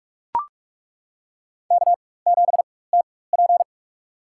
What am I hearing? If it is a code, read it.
Morse code, heard as a keyed tone, about 42 wpm, spaced as for 20 wpm, 700 Hz. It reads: K7TP
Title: USA Call Signs - 4 Characters - 4x Character Spacing - 45wpm